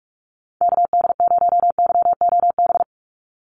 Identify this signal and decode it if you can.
Morse code, heard as a keyed tone, about 44 wpm, 700 Hz. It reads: KD0YOB
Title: NA SOTA Activator Callsigns 45wpm